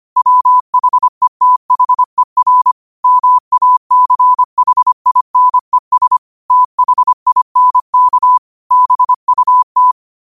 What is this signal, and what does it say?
Morse code, heard as a keyed tone, about 25 wpm, 1000 Hz. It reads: WHETHER MACHINES THINK BUT
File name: qotdfiles / 25wpm